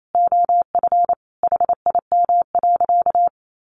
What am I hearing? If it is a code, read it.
Morse code, heard as a keyed tone, about 28 wpm, 700 Hz. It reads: OF HIM.
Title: Sentences from Top 700 Words 28wpm